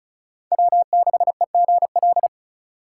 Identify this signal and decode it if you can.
Morse code, heard as a keyed tone, about 45 wpm, 700 Hz. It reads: W6EGL